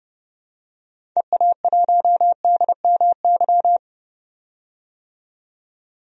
Morse code, heard as a keyed tone, about 30 wpm, 700 Hz. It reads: EA1DMY
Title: DXCC-100 Call Signs - 6 Characters - 30wpm